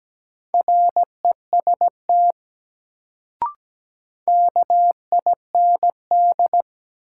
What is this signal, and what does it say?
Morse code, heard as a keyed tone, about 17 wpm, 700 Hz. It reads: REST KIND
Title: Sets of 2 Words - New Word Drill - 17wpm